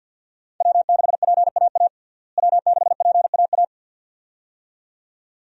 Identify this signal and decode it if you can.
Morse code, heard as a keyed tone, about 48 wpm, 700 Hz. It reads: W6PAA W6PAA
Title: USA Call Signs - 5 Characters - Repeated 2x - 50wpm